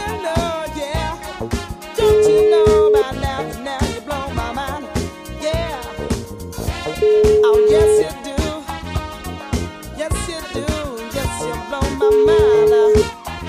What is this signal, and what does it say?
Ringback tone (Europe, 425 Hz), with noise.